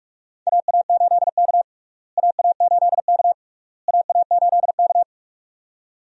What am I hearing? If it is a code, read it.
Morse code, heard as a keyed tone, about 46 wpm, 700 Hz. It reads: AA8K AA8K AA8K
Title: USA Call Signs - 4 Characters - Repeated 3x - 45wpm